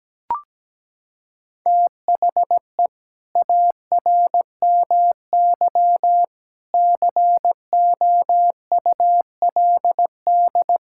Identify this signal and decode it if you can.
Morse code, heard as a keyed tone, about 17 wpm, 700 Hz. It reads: THE ARMY COULD
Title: Sentences from Top 1000 Words - Review Part 5 - 17wpm